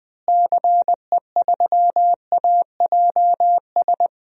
Morse code, heard as a keyed tone, about 20 wpm, 700 Hz. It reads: CE3AJS
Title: DXCC-100 Call Signs - 6 Characters - Repeated 3x - 20wpm